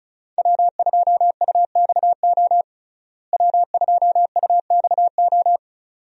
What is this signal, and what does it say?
Morse code, heard as a keyed tone, about 36 wpm, 700 Hz. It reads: W2UXO W2UXO
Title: USA Call Signs - 5 Characters - Repeated 2x - 35wpm